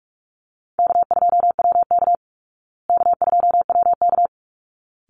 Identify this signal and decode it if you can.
Morse code, heard as a keyed tone, about 44 wpm, 700 Hz. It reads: K2WK K2WK
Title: Easy Call Signs Repeated 2x - 45wpm